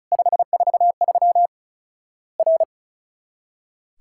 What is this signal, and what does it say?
Morse code, heard as a keyed tone, about 36 wpm, 700 Hz. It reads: 543 R
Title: Worked All Europe DX Contest - QTC Exchange - 35wpm